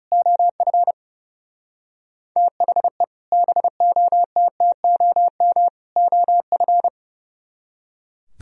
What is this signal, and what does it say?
Morse code, heard as a keyed tone, about 37 wpm, 700 Hz. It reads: OF THE BOTTOM OF
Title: Speed-Racing - Sets of 3 Words 30wpm